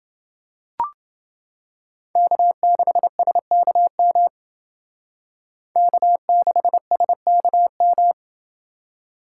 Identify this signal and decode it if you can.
Morse code, heard as a keyed tone, about 29 wpm, 700 Hz. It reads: K6SKM K6SKM